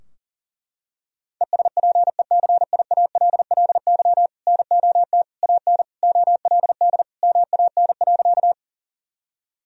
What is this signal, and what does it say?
Morse code, heard as a keyed tone, about 37 wpm, 700 Hz. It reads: ESPECIALLY NOT AN OLD MAN.
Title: Sentences from Top 1000 Words - Review Part 4 - 40wpm